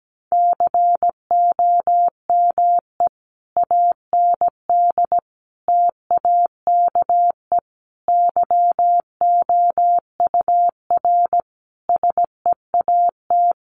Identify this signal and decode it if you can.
Morse code, heard as a keyed tone, about 17 wpm, 700 Hz. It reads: COME AND TAKE YOUR SEAT
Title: Sentences from Top 1500 Words 17wpm